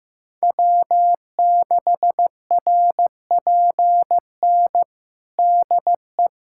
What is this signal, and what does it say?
Morse code, heard as a keyed tone, about 15 wpm, 700 Hz. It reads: W6RPN DE
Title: USA Callsign DE Callsign - 5 Characters - 15wpm